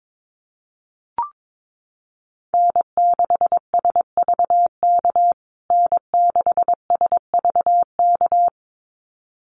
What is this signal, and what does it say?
Morse code, heard as a keyed tone, about 22 wpm, 700 Hz. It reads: N6SVK N6SVK
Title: USA Call Signs Repeated 2x 22wpm